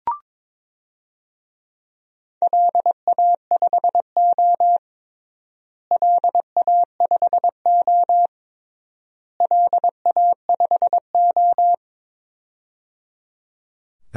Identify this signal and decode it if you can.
Morse code, heard as a keyed tone, about 22 wpm, 700 Hz. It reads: LA5O LA5O LA5O
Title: DXCC-100 Call Signs - 4 Characters - Repeated 3x - 22wpm